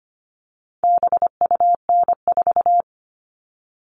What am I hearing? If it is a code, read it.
Morse code, heard as a keyed tone, about 25 wpm, 700 Hz. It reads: BUN4
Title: All Cut Numbers - 4-Digit 25wpm